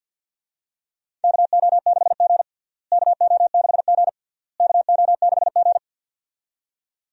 Morse code, heard as a keyed tone, about 51 wpm, 700 Hz. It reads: KO6G KO6G KO6G
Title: USA Call Signs - 4 Characters - Repeated 3x - 50wpm